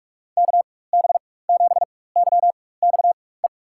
Morse code, heard as a keyed tone, about 42 wpm, spaced as for 20 wpm, 700 Hz. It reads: KB7YXE
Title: USA Call Signs - 6 Characters - 4x Character Spacing - 45wpm